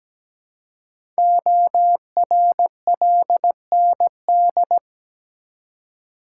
Morse code, heard as a keyed tone, about 17 wpm, 700 Hz. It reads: ORLND
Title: Code Groups - 5 Letters - 17wpm